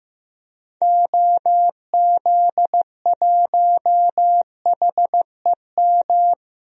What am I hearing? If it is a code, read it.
Morse code, heard as a keyed tone, about 15 wpm, 700 Hz. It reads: OZ1HEM
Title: DXCC-100 Call Signs - 6 Characters - 15wpm